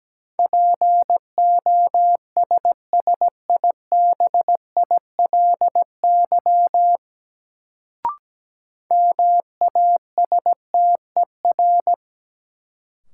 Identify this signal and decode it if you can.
Morse code, heard as a keyed tone, about 17 wpm, 700 Hz. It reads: POSSIBILY MASTER
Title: Sentences from Top 1300 Words - New Word Drill - 17wpm